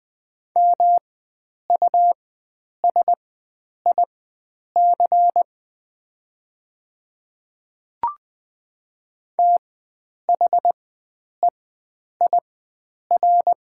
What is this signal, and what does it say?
Morse code, heard as a keyed tone, about 20 wpm, spaced as for 9 wpm, 700 Hz. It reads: MUSIC THEIR
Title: 4x Character Spacing - Top 300 Words - 20wpm